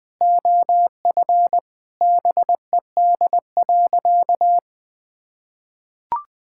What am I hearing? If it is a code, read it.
Morse code, heard as a keyed tone, about 20 wpm, 700 Hz. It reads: OF BED.
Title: Sentences with 3-Word Phrases - 20wpm